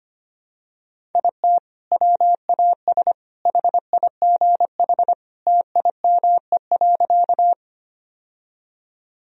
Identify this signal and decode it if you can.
Morse code, heard as a keyed tone, about 25 wpm, 700 Hz. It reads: IT WAS HIGH TIME.
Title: Sentences from Top 1000 Words - Review Part 1 - 25wpm